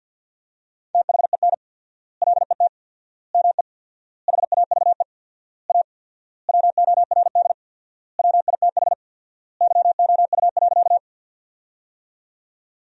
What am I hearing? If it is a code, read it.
Morse code, heard as a keyed tone, about 48 wpm, 700 Hz. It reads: THEN LET ME HAVE A WORD WITH YOU.
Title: 4x Word-Spacing - Sentences from Top 300 Words - 50wpm